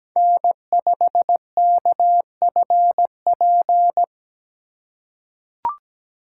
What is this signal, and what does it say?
Morse code, heard as a keyed tone, about 17 wpm, 700 Hz. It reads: N5KFP
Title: USA Call Signs - 5 Characters - 3x Character Spacing - 17wpm